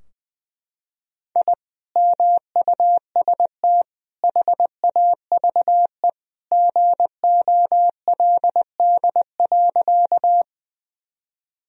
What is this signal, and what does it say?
Morse code, heard as a keyed tone, about 20 wpm, 700 Hz. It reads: I MUST HAVE GOLD.
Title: Sentences from Top 1000 Words - Review Part 3 - 20wpm